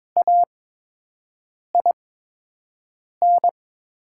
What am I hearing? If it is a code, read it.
Morse code, heard as a keyed tone, about 22 wpm, spaced as for 6 wpm, 690 Hz. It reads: AIN
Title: 8x Character Spacing - Top 300 Words - 22wpm